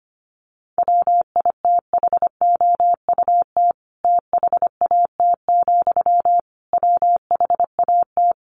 Morse code, heard as a keyed tone, about 25 wpm, 700 Hz. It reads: WITHOUT THAT, WHAT
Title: Sentences from Top 600 Words 25wpm